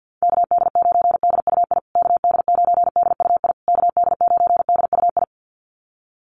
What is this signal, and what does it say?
Morse code, heard as a keyed tone, about 50 wpm, 700 Hz. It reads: KD9DUI KD9DUI KD9DUI
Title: USA Call Signs Repeated 3x 50wpm